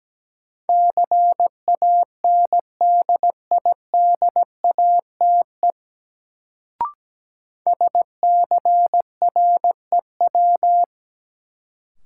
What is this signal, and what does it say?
Morse code, heard as a keyed tone, about 17 wpm, 700 Hz. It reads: CANDIDATE SCREW
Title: Sentences from Top 1600 Words - New Word Drill - 17wpm